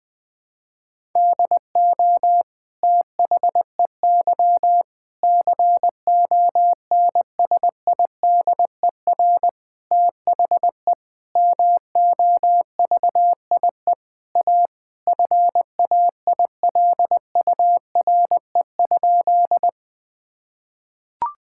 Morse code, heard as a keyed tone, about 20 wpm, 700 Hz. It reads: DO THEY CONSIDER THE MOVIE A FAILURE?
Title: Sentences from Top 1000 Words - Review Part 6 - 20wpm